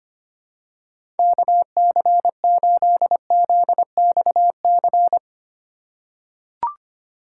Morse code, heard as a keyed tone, about 25 wpm, 700 Hz. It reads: KC8ZXC
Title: USA Call Signs - 4x Character Spacing - 25wpm